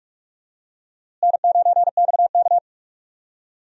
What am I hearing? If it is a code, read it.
Morse code, heard as a keyed tone, about 41 wpm, 700 Hz. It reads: N9XK
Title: USA Call Signs - 4 Characters - Repeated 2x - 45wpm